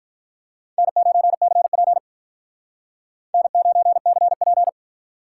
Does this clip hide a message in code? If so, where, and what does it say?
Yes: Morse code from 0.8 to 4.7 s, heard as a keyed tone, about 45 wpm, 700 Hz. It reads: N9CP N9CP